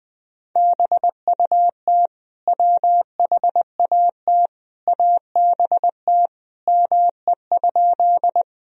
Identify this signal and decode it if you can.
Morse code, heard as a keyed tone, about 20 wpm, 700 Hz. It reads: BUT WHAT ABT ME?
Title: Sentences from Top 200 Words and Abbreviations 20wpm